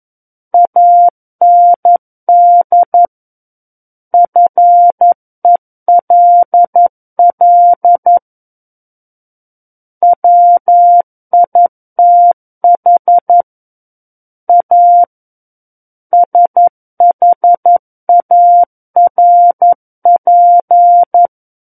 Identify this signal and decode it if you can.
Morse code, heard as a keyed tone, about 11 wpm, 700 Hz. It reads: AND FELL WITH A SHARP
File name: Book One / 02-Chapter 2 - 11 WPM